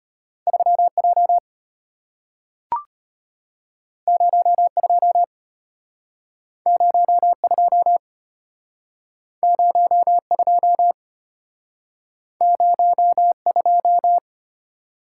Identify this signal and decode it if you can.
Morse code, heard as a keyed tone, about 33 wpm, 700 Hz. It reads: 3J 02 02 02 02